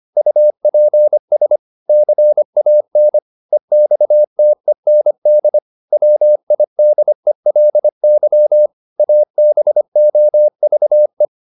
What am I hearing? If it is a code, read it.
Morse code, heard as a keyed tone, about 25 wpm, 600 Hz. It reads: UPS CAN EXTEND WIDELY ABOVE